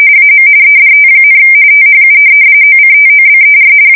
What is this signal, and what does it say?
RTTY, about 45.45 baud. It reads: AM EDT, THU OCTOBER
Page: 170 Shift 2125/2295